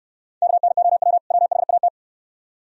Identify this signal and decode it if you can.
Morse code, heard as a keyed tone, about 66 wpm, 700 Hz. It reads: KNOW WHAT